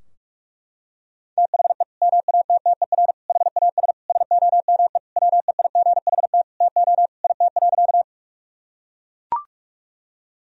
Morse code, heard as a keyed tone, about 42 wpm, 700 Hz. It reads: THE MATTER HAS SOME WEIGHT TO IT.
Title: Sentences from Top 1000 Words - Review Part 3 - 45wpm